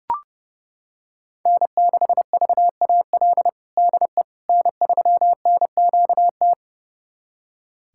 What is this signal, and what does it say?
Morse code, heard as a keyed tone, about 30 wpm, 700 Hz. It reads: N6VAL DE N3NQT
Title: USA Callsign DE Callsign - 5 Characters - 30wpm